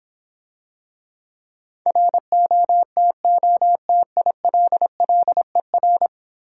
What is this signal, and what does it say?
Morse code, heard as a keyed tone, about 26 wpm, 700 Hz. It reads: ROTOTILLER